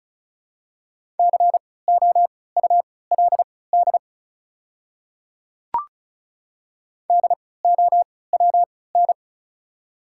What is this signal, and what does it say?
Morse code, heard as a keyed tone, about 34 wpm, spaced as for 20 wpm, 690 Hz. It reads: COULD DOWN
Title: 3x Character Spacing - Top 100 Words - 35wpm